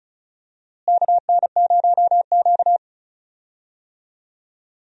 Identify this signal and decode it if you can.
Morse code, heard as a keyed tone, about 36 wpm, 700 Hz. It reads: KN0Q
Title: USA Call Signs - 4 Characters - Repeated 3x - 35wpm